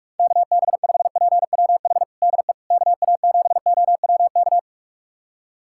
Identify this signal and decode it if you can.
Morse code, heard as a keyed tone, about 47 wpm, 700 Hz. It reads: KB5PWH DE KA7OWK
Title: USA Callsign DE Callsign 45wpm